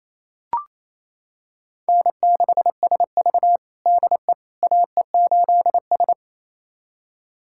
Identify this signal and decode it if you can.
Morse code, heard as a keyed tone, about 28 wpm, 700 Hz. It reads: N6SV DE AE8S
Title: USA Callsign DE Callsign - 4 Characters - 28wpm